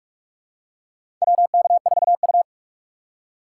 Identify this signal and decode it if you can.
Morse code, heard as a keyed tone, about 46 wpm, 700 Hz. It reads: WK4U
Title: USA Call Signs - 6x Character Spacing - 45wpm